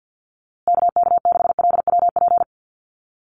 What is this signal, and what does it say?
Morse code, heard as a keyed tone, about 48 wpm, 700 Hz. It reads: KK6LWP
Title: USA Call Signs Repeated 2x 50wpm